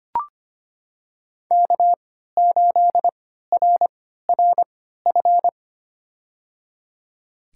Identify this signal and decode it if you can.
Morse code, heard as a keyed tone, about 25 wpm, spaced as for 14 wpm, 700 Hz. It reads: K8RRF
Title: USA Call Signs - 5 Characters - 3x Character Spacing - 25wpm